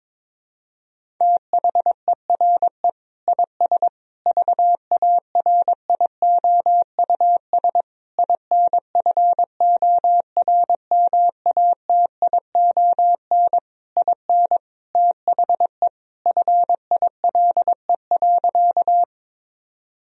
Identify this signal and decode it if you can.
Morse code, heard as a keyed tone, about 22 wpm, 700 Hz. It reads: THERE IS VARIOUS INFORMATION IN THE FILE.
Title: Sentences from Top 1000 Words - Review Part 4 - 22wpm